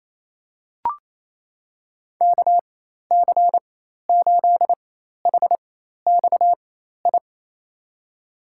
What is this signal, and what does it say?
Morse code, heard as a keyed tone, about 28 wpm, spaced as for 13 wpm, 700 Hz. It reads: KC8HXI
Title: USA Call Signs - 6 Characters - 4x Character Spacing - 28wpm